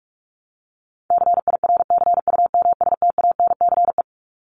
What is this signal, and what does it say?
Morse code, heard as a keyed tone, about 44 wpm, 700 Hz. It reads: CIRCUMSTANCE
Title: ES:S Suffix - Top 100 Words with Base Words 45wpm